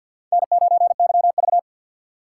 Morse code, heard as a keyed tone, about 47 wpm, 700 Hz. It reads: N9YV
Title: USA Call Signs - 4 Characters - 3x Character Spacing - 50wpm